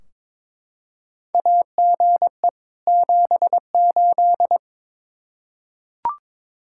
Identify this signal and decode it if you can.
Morse code, heard as a keyed tone, about 21 wpm, 700 Hz. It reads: AGE 78
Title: Practice QSO - Exchange 3 - More Info - 22wpm